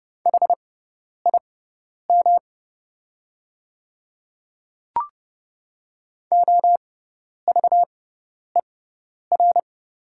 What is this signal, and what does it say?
Morse code, heard as a keyed tone, about 30 wpm, spaced as for 10 wpm, 700 Hz. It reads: HIM OVER
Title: 6x Character Spacing - Top 100 Words - 30wpm